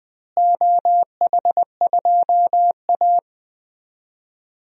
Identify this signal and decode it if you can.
Morse code, heard as a keyed tone, about 21 wpm, 700 Hz. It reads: OH2A
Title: DXCC-100 Call Signs - 4 Characters - Repeated 3x - 20wpm